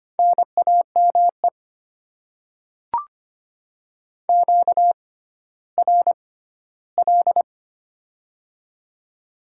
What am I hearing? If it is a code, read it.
Morse code, heard as a keyed tone, about 25 wpm, spaced as for 14 wpm, 700 Hz. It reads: NAME QRL
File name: Top 20 QSO Elements - 6x Character Spacing - 25wpm